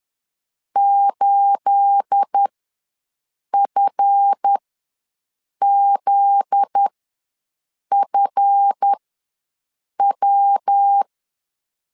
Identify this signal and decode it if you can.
Morse code, heard as a keyed tone, about 10 wpm, spaced as for 6 wpm, 800 Hz. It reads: 8FZFW
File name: LEZIONI / 10 SIGNS / 10A SLASH ws 10wpm 4121